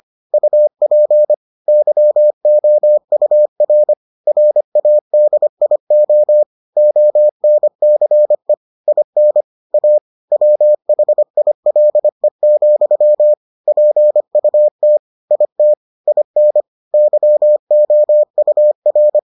Morse code, heard as a keyed tone, about 25 wpm, 600 Hz. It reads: UP YOUR RADIO ONCE IN A WHILE, PUT IT IN YOUR